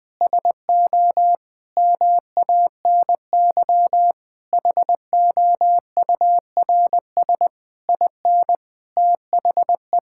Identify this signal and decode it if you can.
Morse code, heard as a keyed tone, about 21 wpm, 700 Hz. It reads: SO MANY HOURS IN THE
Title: Sets of 9 Words - 20wpm